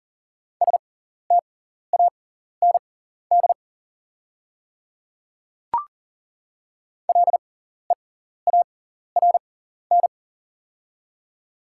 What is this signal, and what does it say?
Morse code, heard as a keyed tone, about 40 wpm, spaced as for 14 wpm, 700 Hz. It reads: STAND LEARN
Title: 6x Character Spacing - Top 300 Words - 40wpm